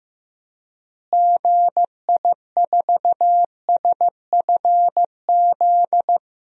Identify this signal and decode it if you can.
Morse code, heard as a keyed tone, about 15 wpm, 700 Hz. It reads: GI4SFZ
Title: DXCC-100 Call Signs - 6 Characters - Repeated 2x - 15wpm